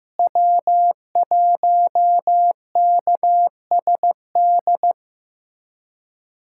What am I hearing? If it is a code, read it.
Morse code, heard as a keyed tone, about 15 wpm, 700 Hz. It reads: W1KSD
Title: USA Callsign DE Callsign 15wpm